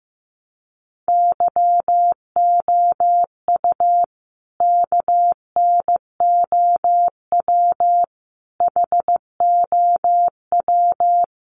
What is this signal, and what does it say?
Morse code, heard as a keyed tone, about 14 wpm, 700 Hz. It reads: YOU KNOW HOW
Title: Sentences from Top 1100 Words 15wpm